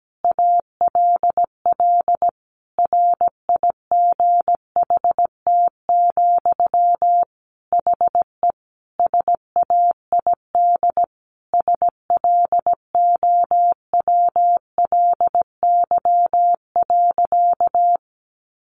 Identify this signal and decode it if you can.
Morse code, heard as a keyed tone, about 17 wpm, 700 Hz. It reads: ALL RIGHT, HE SAID SLOWLY.
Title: Sentences from Top 1100 Words 17wpm